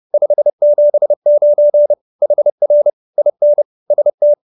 Morse code, heard as a keyed tone, about 30 wpm, 590 Hz. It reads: 579 HR IN ST
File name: QSO314 30